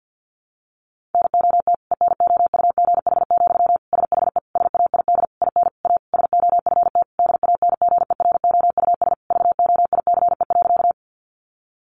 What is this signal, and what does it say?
Morse code, heard as a keyed tone, about 50 wpm, 700 Hz. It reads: NOT ENOUGH, SHE SAID IN A SOFT DANGEROUS VOICE.
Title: Sentences from Top 1500 Words 50wpm